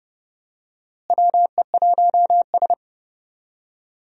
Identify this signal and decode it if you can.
Morse code, heard as a keyed tone, about 30 wpm, 700 Hz. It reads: WE1S
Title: USA Call Signs - 4 Characters - Repeated 2x - 30wpm